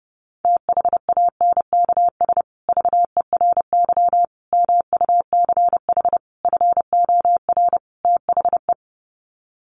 Morse code, heard as a keyed tone, about 30 wpm, 700 Hz. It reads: THANKS VERY MUCH FOR THE
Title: Sets of 5 Words Encore2 30wpm